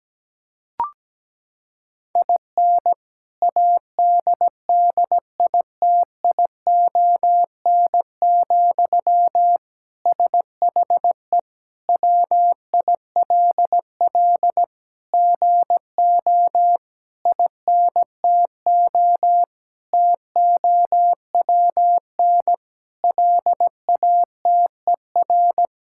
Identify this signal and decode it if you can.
Morse code, heard as a keyed tone, about 17 wpm, 700 Hz. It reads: IN ADDITION, SHE WILL GO INTO TOWN LATER
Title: Sentences from Top 1000 Words - Review Part 6 - 17wpm